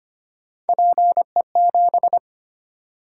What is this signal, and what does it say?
Morse code, heard as a keyed tone, about 25 wpm, 700 Hz. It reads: PE7
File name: Three Letter-Number Combinations - 25wpm